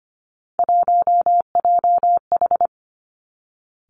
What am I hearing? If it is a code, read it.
Morse code, heard as a keyed tone, about 25 wpm, 700 Hz. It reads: 1JH
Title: Three Challenging Letter-Number Combinations 25wpm